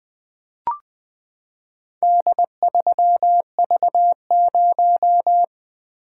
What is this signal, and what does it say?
Morse code, heard as a keyed tone, about 20 wpm, 700 Hz. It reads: D3V0
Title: Four Letter-Number Combinations - 20wpm